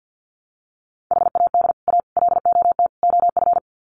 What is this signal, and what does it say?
Morse code, heard as a keyed tone, about 50 wpm, 700 Hz. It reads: HAD A LOT OF
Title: Sets of 4 Words 50wpm